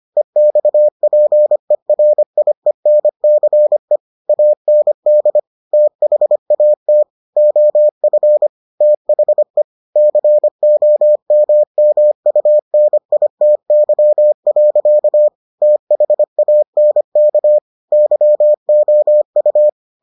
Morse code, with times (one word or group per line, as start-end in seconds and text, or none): 0.2-4.0 EXPERIENCE
4.3-5.4 AND
5.7-7.0 THAT
7.4-8.5 OF
8.8-9.6 THE
10.0-15.3 COMMUNITY.
15.6-17.6 THANK
17.9-19.7 YOU